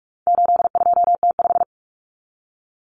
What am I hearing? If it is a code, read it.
Morse code, heard as a keyed tone, about 44 wpm, 700 Hz. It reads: 82T5
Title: Common Cut Numbers - 4-Digit 45wpm